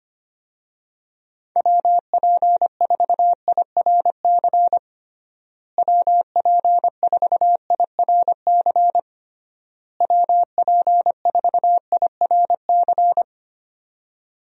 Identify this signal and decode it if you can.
Morse code, heard as a keyed tone, about 25 wpm, 700 Hz. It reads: WP4IRC WP4IRC WP4IRC
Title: USA Call Signs - 6 Characters - Repeated 3x - 25wpm